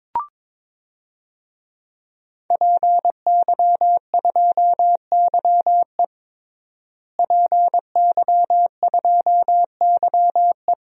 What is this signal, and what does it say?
Morse code, heard as a keyed tone, about 22 wpm, 700 Hz. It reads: PY2YE PY2YE
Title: DXCC-100 Call Signs - 5 Characters - Repeated 2x - 22wpm